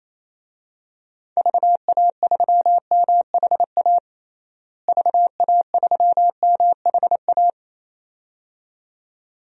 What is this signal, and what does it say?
Morse code, heard as a keyed tone, about 28 wpm, 700 Hz. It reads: VA3MHA VA3MHA
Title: DXCC-100 Call Signs - 6 Characters - Repeated 2x - 28wpm